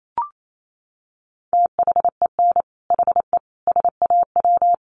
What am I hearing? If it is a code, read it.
Morse code, heard as a keyed tone, about 28 wpm, 700 Hz. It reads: THEN HE SAW
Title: Sentences from Top 400 Words 28wpm